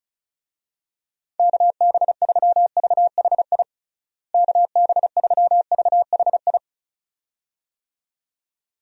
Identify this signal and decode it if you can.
Morse code, heard as a keyed tone, about 34 wpm, 700 Hz. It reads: KB3VHI KB3VHI
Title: USA Call Signs - 6 Characters - Repeated 2x - 35wpm